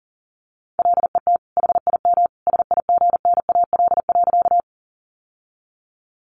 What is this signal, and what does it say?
Morse code, heard as a keyed tone, about 37 wpm, 700 Hz. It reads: LET HIM SIGNAL.
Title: Sentences from Top 1400 Words 40wpm